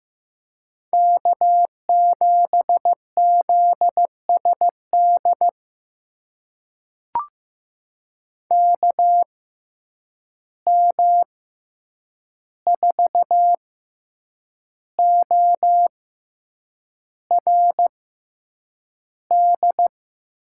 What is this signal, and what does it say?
Morse code, heard as a keyed tone, about 15 wpm, spaced as for 7 wpm, 700 Hz. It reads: K7ZSD KM4ORD
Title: USA Call Signs - 6x Character Spacing - 15wpm